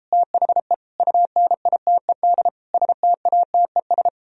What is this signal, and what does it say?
Morse code, heard as a keyed tone, about 33 wpm, 700 Hz. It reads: THE UNITED STATES